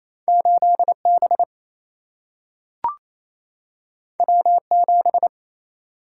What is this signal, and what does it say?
Morse code, heard as a keyed tone, about 28 wpm, 700 Hz. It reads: 8 B W 7
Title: Single-Letter Single-Number 28wpm